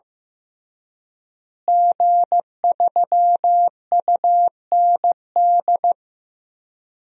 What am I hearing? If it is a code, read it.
Morse code, heard as a keyed tone, about 16 wpm, 700 Hz. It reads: G3UND
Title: DXCC-100 Call Signs - 5 Characters - Repeated 3x - 15wpm